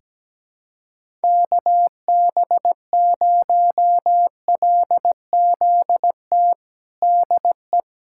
Morse code, heard as a keyed tone, about 17 wpm, 700 Hz. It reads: KB0LZT DE
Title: USA Callsign DE Callsign - 6 Characters - 17wpm